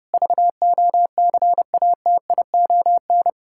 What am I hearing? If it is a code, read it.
Morse code, heard as a keyed tone, about 30 wpm, 700 Hz. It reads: VOCATION
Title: Pangram Challenge v2 - 30wpm